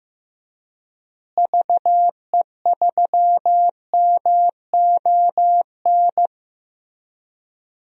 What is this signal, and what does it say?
Morse code, heard as a keyed tone, about 15 wpm, 700 Hz. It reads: VE3MON
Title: DXCC-100 Call Signs - 6 Characters - Repeated 3x - 15wpm